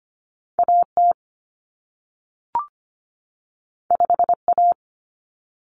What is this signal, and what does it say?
Morse code, heard as a keyed tone, about 25 wpm, 700 Hz. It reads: AT 5A
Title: Common Cut Numbers - 2-Digit 25wpm